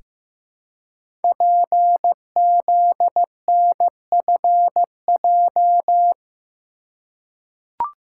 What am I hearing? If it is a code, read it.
Morse code, heard as a keyed tone, about 15 wpm, 700 Hz. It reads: PZNFJ
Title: Code Groups - 5 Letters - 15wpm